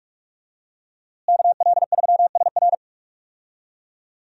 Morse code, heard as a keyed tone, about 46 wpm, 700 Hz. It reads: KL3SR
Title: USA Call Signs - 5 Characters - Repeated 3x - 45wpm